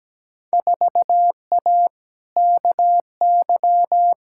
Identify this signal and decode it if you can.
Morse code, heard as a keyed tone, about 17 wpm, 700 Hz. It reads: 4A KY
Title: Field Day Exchanges Encore - 17wpm